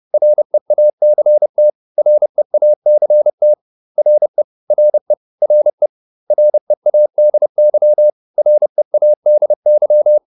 Morse code, heard as a keyed tone, about 30 wpm, 600 Hz. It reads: REACT REACT RE RE RE READY READY